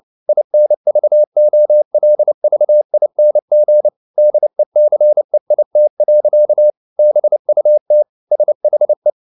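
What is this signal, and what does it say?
Morse code, heard as a keyed tone, about 29 wpm, 600 Hz. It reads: INVOLVING DECEIT. BUT SHE
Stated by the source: Morse code at 29 wpm, sending Ken Follett's novel Never in English, from CW med Gnister.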